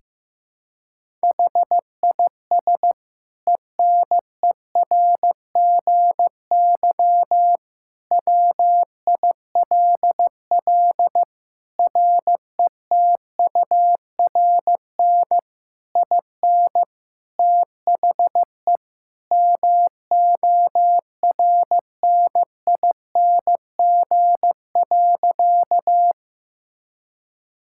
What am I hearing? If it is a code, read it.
Morse code, heard as a keyed tone, about 15 wpm, 700 Hz. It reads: HIS ENERGY WILL RETURN IN THE MORNING.
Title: Sentences from Top 1000 Words - Review Part 6 - 15wpm